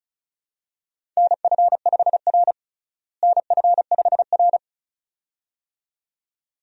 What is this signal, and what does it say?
Morse code, heard as a keyed tone, about 36 wpm, 700 Hz. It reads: NF5R NF5R
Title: USA Call Signs - 4 Characters - Repeated 2x - 35wpm